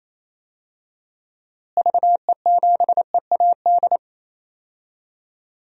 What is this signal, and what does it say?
Morse code, heard as a keyed tone, about 28 wpm, 700 Hz. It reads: VE7EAD
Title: DXCC-100 Call Signs - 6 Characters - Repeated 3x - 28wpm